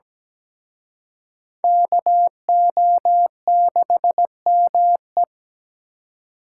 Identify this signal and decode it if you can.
Morse code, heard as a keyed tone, about 20 wpm, 700 Hz. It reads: KO6ME
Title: Speed Racing - USA 2x2 Call Signs 17wpm